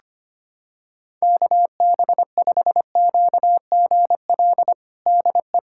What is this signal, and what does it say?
Morse code, heard as a keyed tone, about 25 wpm, 700 Hz. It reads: KB5QGL DE
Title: USA Callsign DE Callsign - 6 Characters - 25wpm